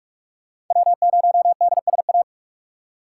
Morse code, heard as a keyed tone, about 45 wpm, 700 Hz. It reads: W0DSA
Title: USA Call Signs - 3x Character Spacing - 45wpm